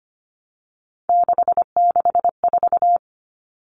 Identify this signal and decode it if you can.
Morse code, heard as a keyed tone, about 25 wpm, 700 Hz. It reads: 664
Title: Common Cut Numbers - 3-Digit 25wpm